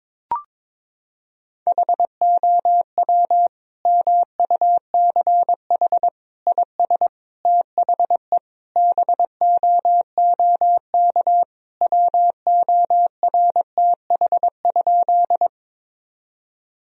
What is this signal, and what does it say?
Morse code, heard as a keyed tone, about 22 wpm, 700 Hz. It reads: HOW MUCH IS THE BOOK WORTH?
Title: Sentences from Top 1000 Words - Review Part 1 - 22wpm